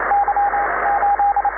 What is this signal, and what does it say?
Morse code, heard as a keyed tone, about 25 wpm, 810 Hz, with radio noise. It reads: Y 8
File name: iw0ffk / 2005 05 31 ey8mm iw0ffk cw